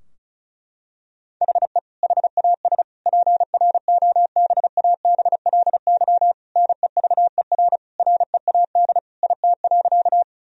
Morse code, heard as a keyed tone, about 36 wpm, 700 Hz. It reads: HE HAS PROBABLY NEVER READ IT.
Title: Sentences from Top 1000 Words - Review Part 4 - 35wpm